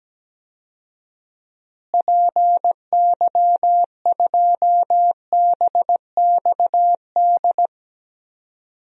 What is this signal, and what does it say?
Morse code, heard as a keyed tone, about 17 wpm, 700 Hz. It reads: PY2BXD
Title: DXCC-100 Call Signs - 6 Characters - Repeated 2x - 17wpm